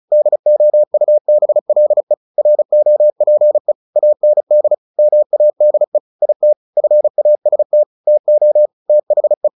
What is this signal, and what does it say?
Morse code, heard as a keyed tone, about 34 wpm, 600 Hz. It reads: DOUBLE ROPE AND MADE IT FAST TO THE
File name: the old man and the sea 2 35wpm